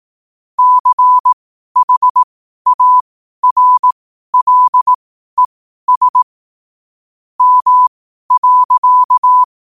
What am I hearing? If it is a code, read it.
Morse code, heard as a keyed tone, about 18 wpm, spaced as for 12 wpm, 1000 Hz. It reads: CHARLES M.
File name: qotdfiles / 12wpm